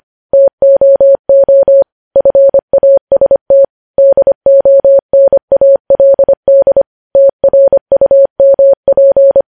Morse code, heard as a keyed tone, about 25 wpm, 550 Hz. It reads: TOO FAST DONALD TRUMP
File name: podcast / News headlines-1x-25x25